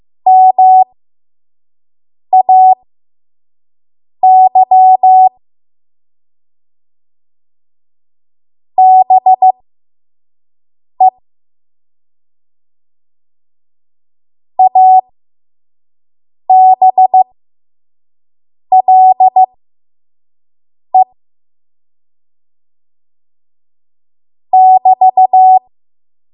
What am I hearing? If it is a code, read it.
Morse code, heard as a keyed tone, about 15 wpm, spaced as for 5 wpm, 750 Hz. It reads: MAY BE ABLE <BT>
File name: Morse / 05WPM